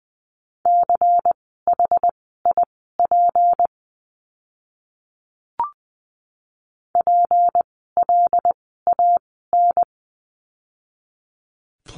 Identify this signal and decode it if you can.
Morse code, heard as a keyed tone, about 20 wpm, spaced as for 14 wpm, 700 Hz. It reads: CHIP PLAN
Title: 153 Lesson 25f - Letter p - Words 20wpm